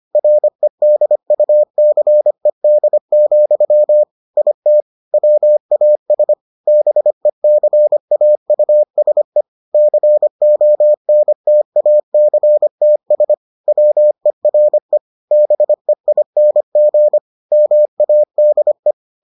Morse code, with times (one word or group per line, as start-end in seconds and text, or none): 0.1-4.0 REDUCED,
4.4-4.8 IT
5.1-6.3 WAS
6.7-9.4 BECAUSE
9.7-13.3 CONTACTS
13.7-15.0 WERE
15.3-17.2 BEING
17.5-18.9 MADE